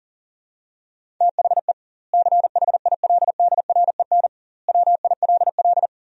Morse code, heard as a keyed tone, about 40 wpm, 700 Hz. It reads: THE CHILDREN WILL
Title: 2x Word-Spacing - Sentences from Top 300 Words - 40wpm